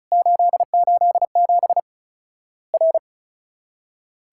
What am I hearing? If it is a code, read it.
Morse code, heard as a keyed tone, about 36 wpm, 700 Hz. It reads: 887 R
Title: Worked All Europe DX Contest - QTC Exchange - 35wpm